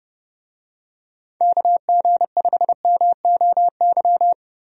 Morse code, heard as a keyed tone, about 31 wpm, 700 Hz. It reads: KG5MOY
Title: USA Call Signs - 6 Characters - Repeated 2x - 30wpm